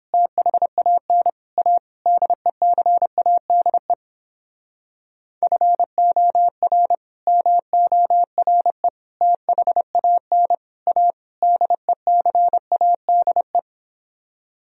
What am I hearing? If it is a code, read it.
Morse code, heard as a keyed tone, about 28 wpm, 700 Hz. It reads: THAN A DECADE FOR MORE THAN A DECADE